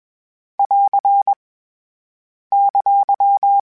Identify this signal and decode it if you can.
Morse code, heard as a keyed tone, about 21 wpm, 800 Hz. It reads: <AR> !